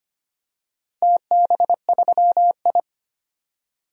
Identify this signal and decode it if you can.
Morse code, heard as a keyed tone, about 25 wpm, 700 Hz. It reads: TB3I
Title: DXCC-100 Call Signs - 4 Characters - 25wpm